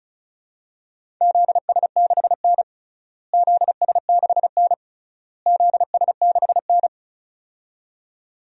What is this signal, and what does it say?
Morse code, heard as a keyed tone, about 36 wpm, 700 Hz. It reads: ZS6N ZS6N ZS6N
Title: DXCC-100 Call Signs - 4 Characters - Repeated 3x - 35wpm